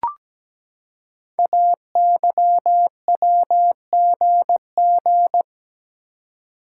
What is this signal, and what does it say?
Morse code, heard as a keyed tone, about 17 wpm, 700 Hz. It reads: AYWGG
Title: Code Groups - 5 Letters - 17wpm